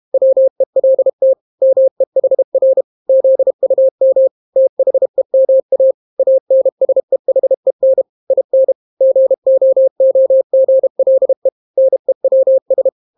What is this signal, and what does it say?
Morse code, heard as a keyed tone, about 32 wpm, 530 Hz. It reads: WELT MEHR ZUM THEMA ANSEHEN IN GOOGLE NEWS